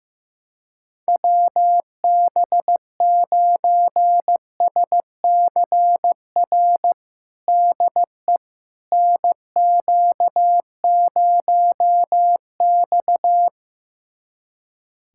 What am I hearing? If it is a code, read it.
Morse code, heard as a keyed tone, about 15 wpm, 700 Hz. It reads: WB9SCR DE NQ0X
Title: USA Callsign DE Callsign 15wpm